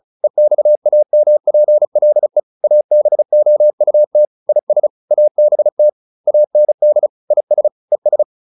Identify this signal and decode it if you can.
Morse code, heard as a keyed tone, about 34 wpm, 600 Hz. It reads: EXAMPLE ABOUT IS ABT AND IS ES